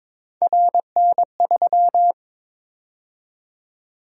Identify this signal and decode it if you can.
Morse code, heard as a keyed tone, about 22 wpm, 700 Hz. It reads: RN3
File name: practice / 22wpm / Three Letter-Number Combinations - 22wpm